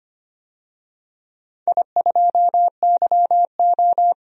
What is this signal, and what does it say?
Morse code, heard as a keyed tone, about 25 wpm, 690 Hz. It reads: I2YO
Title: DXCC-100 Call Signs - 4 Characters - 25wpm